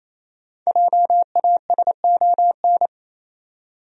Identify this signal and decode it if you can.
Morse code, heard as a keyed tone, about 28 wpm, 700 Hz. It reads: JASON
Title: USA Common Names - 28wpm